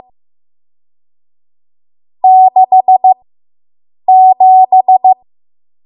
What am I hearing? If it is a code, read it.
Morse code, heard as a keyed tone, about 15 wpm, spaced as for 7 wpm, 750 Hz. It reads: 67